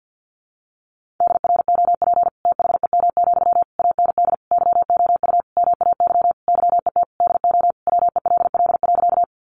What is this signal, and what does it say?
Morse code, heard as a keyed tone, about 50 wpm, 700 Hz. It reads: DROP THEM, AND YOU MAY YET DO WELL.
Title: Sentences from Top 1400 Words 50wpm